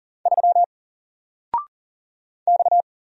Morse code, heard as a keyed tone, about 38 wpm, 700 Hz. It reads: 3 X
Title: Single Letter-Number 40wpm